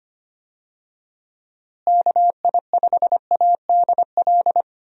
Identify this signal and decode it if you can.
Morse code, heard as a keyed tone, about 25 wpm, 700 Hz. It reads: KI5ADL
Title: USA Call Signs - 6 Characters - Repeated 3x - 25wpm